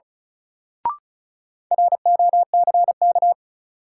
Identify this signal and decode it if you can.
Morse code, heard as a keyed tone, about 36 wpm, 700 Hz. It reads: ROCK
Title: Sentences from Top 400 Words - New Word Drill - 35wpm